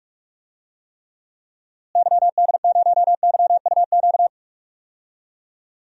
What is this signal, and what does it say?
Morse code, heard as a keyed tone, about 46 wpm, 700 Hz. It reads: YD0YUQ
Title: DXCC-100 Call Signs - 6 Characters - Repeated 2x - 45wpm